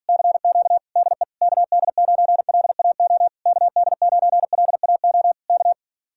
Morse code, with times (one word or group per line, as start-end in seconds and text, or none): 0.1-0.8 CQ
1.0-1.2 DE
1.4-3.3 KD9LAO
3.5-5.3 KD9LAO
5.5-5.7 K